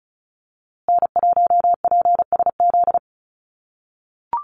Morse code, heard as a keyed tone, about 34 wpm, 700 Hz. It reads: N1PSZ
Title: USA Call Signs Repeated 3x 35wpm